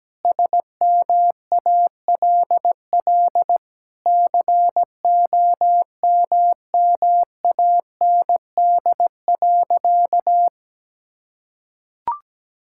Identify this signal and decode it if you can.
Morse code, heard as a keyed tone, about 17 wpm, 700 Hz. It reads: SMALL COMMAND.
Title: Sentences from Top 1600 Words 17wpm